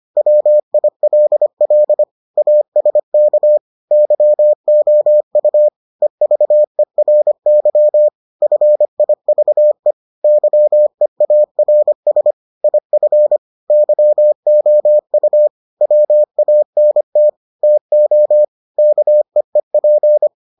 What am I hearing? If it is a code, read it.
Morse code, heard as a keyed tone, about 25 wpm, 600 Hz. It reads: WILL ASK YOU EVERY FIVE YEARS IF YOU WANT TO KEEP